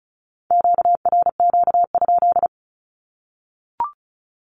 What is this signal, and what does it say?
Morse code, heard as a keyed tone, about 34 wpm, 700 Hz. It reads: QRQ?
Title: Q-Codes 35wpm